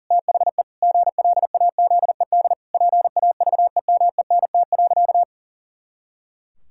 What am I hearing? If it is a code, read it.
Morse code, heard as a keyed tone, about 40 wpm, 700 Hz. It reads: THE GLAZED PAVEMENT.
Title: Pangram Challenge v2 - 40wpm